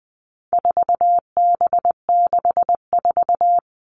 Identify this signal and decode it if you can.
Morse code, heard as a keyed tone, about 20 wpm, 700 Hz. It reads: 4B64
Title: All Cut Numbers - 4-Digit 20wpm